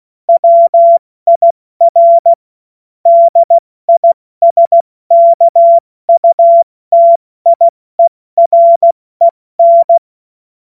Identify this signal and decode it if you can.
Morse code, heard as a keyed tone, about 16 wpm, 680 Hz. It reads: WIR DISKUTIEREN
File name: Morsekurs / Witze / Witz 16. Dezember 2025 70 F80